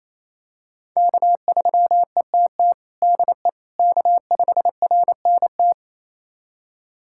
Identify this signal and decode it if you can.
Morse code, heard as a keyed tone, about 28 wpm, 700 Hz. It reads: K3ETT DE K5RNT
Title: USA Callsign DE Callsign - 5 Characters - 28wpm